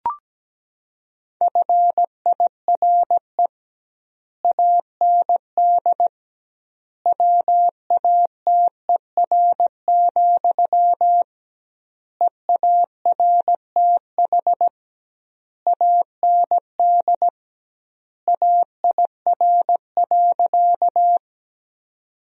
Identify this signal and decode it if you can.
Morse code, heard as a keyed tone, about 17 wpm, 700 Hz. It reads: FIRE AND WATER, EARTH AND AIR.
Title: 2x Word-Spacing - Sentences from Top 400 Words - 17wpm